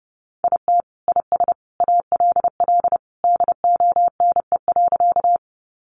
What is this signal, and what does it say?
Morse code, heard as a keyed tone, about 30 wpm, 700 Hz. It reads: IT IS ALL DONE.
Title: Sentences from Top 500 Words 30wpm